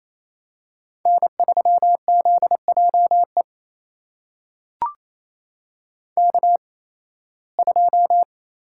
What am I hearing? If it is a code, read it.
Morse code, heard as a keyed tone, about 28 wpm, spaced as for 18 wpm, 700 Hz. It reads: N3ZJE K2
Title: USA Call Signs - 8x Character Spacing - 28wpm